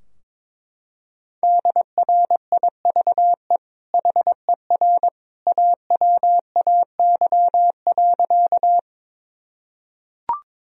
Morse code, heard as a keyed tone, about 22 wpm, 700 Hz. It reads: DRIVE HER AWAY.
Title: Sentences from Top 1000 Words - Review Part 2 - 22wpm